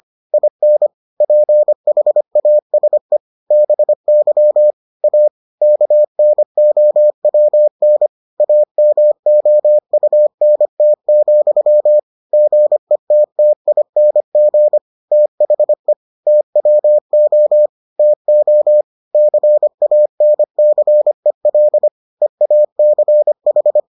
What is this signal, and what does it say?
Morse code, heard as a keyed tone, about 25 wpm, 600 Hz. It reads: IN PHASE BY A KNOWN AMOUNT, GETTING THE TWO TO CANCEL EACH